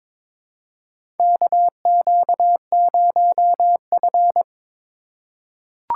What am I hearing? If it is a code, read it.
Morse code, heard as a keyed tone, about 22 wpm, 700 Hz. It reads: KQ0F
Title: USA Call Signs - 4 Characters - 3x Character Spacing - 22wpm